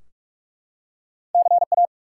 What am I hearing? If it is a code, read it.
Morse code, heard as a keyed tone, about 44 wpm, 700 Hz. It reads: CA
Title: US States and Canadian Provinces 45wpm